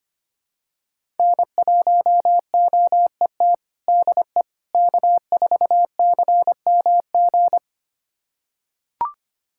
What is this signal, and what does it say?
Morse code, heard as a keyed tone, about 25 wpm, 700 Hz. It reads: N1OET DE K4CMG
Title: USA Callsign DE Callsign - 5 Characters - 25wpm